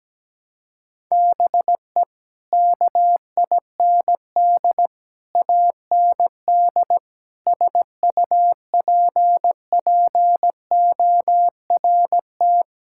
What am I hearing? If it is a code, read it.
Morse code, heard as a keyed tone, about 17 wpm, 700 Hz. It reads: BE KIND AND SUPPORT
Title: Sentences from Top 1000 Words - Review Part 5 - 17wpm